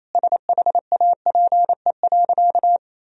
Morse code, heard as a keyed tone, about 28 wpm, 700 Hz. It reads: SHAPE.
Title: Sentences from Top 1000 Words - Review Part 3 - 28wpm